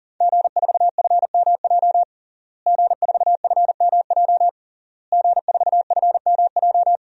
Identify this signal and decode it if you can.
Morse code, heard as a keyed tone, about 40 wpm, 700 Hz. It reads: G4FMJ G4FMJ G4FMJ
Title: DXCC-100 Call Signs - 5 Characters - Repeated 3x - 40wpm